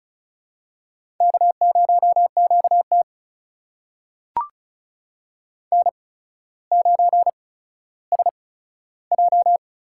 Morse code, heard as a keyed tone, about 33 wpm, spaced as for 15 wpm, 700 Hz. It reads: K0QT N9SJ
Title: USA Call Signs - 4 Characters - 8x Character Spacing - 35wpm